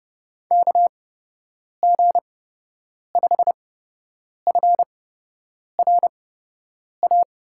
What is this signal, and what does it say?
Morse code, heard as a keyed tone, about 31 wpm, spaced as for 8 wpm, 700 Hz. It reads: KG5FRA
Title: USA Call Signs - 8x Character Spacing - 30wpm